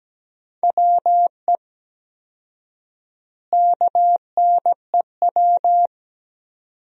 Morse code, heard as a keyed tone, about 17 wpm, 700 Hz. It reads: WE KNEW
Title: 4x Word-Spacing - Sentences from Top 400 Words - 17wpm